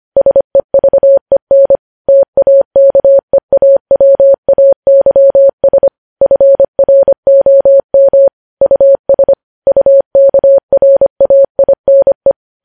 Morse code, heard as a keyed tone, about 25 wpm, 550 Hz. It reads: SEVEN TAKEAWAYS FROM US UKRAINE
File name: podcast / News headlines-1x-25x25